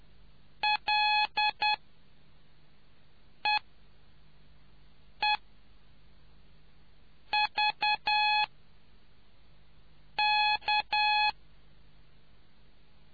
Morse code, heard as a keyed tone, about 10 wpm, spaced as for 4 wpm, 840 Hz. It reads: LEEVK